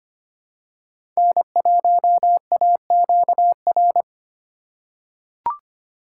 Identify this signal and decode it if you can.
Morse code, heard as a keyed tone, about 25 wpm, 700 Hz. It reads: N1AQR
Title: USA Call Signs - 5 Characters - 3x Character Spacing - 25wpm